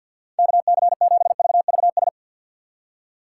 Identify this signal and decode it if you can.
Morse code, heard as a keyed tone, about 64 wpm, 700 Hz. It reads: KC7VVS